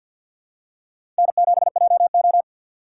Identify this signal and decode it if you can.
Morse code, heard as a keyed tone, about 52 wpm, 700 Hz. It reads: N7JO
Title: USA Call Signs - 4 Characters - 6x Character Spacing - 50wpm